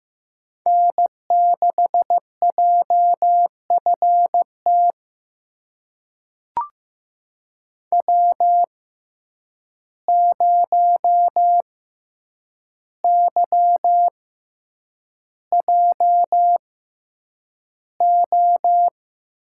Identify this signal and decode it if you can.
Morse code, heard as a keyed tone, about 15 wpm, spaced as for 8 wpm, 700 Hz. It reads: N6JFT W0YJO
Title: USA Call Signs - 5 Characters - 6x Character Spacing - 15wpm